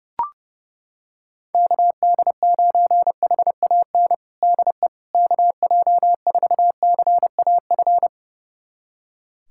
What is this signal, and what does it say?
Morse code, heard as a keyed tone, about 30 wpm, 700 Hz. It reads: KD9HAN DE KJ4CAF
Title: USA Callsign DE Callsign 30wpm